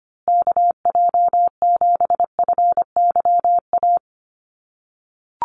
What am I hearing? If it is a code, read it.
Morse code, heard as a keyed tone, about 25 wpm, 700 Hz. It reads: KJ7FYA
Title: USA Call Signs Repeated 2x 25wpm